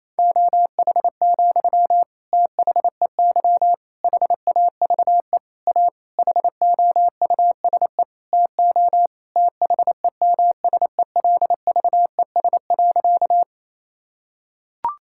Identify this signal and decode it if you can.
Morse code, heard as a keyed tone, about 28 wpm, 700 Hz. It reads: OH, THEY HAVE A HOUSE TO THEMSELVES.
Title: Sentences from Top 1000 Words - Review Part 5 - 28wpm